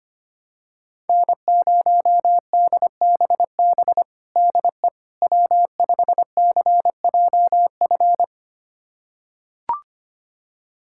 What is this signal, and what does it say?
Morse code, heard as a keyed tone, about 25 wpm, 700 Hz. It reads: N0DBB DE W5CJF
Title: USA Callsign DE Callsign - 5 Characters - 25wpm